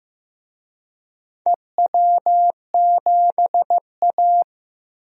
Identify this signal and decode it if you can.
Morse code, heard as a keyed tone, about 16 wpm, 700 Hz. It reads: EW7A
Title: DXCC-100 Call Signs - 4 Characters - Repeated 2x - 15wpm